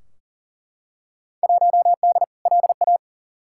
Morse code, heard as a keyed tone, about 40 wpm, 700 Hz. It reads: 1D LA
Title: Field Day Exchanges - 40wpm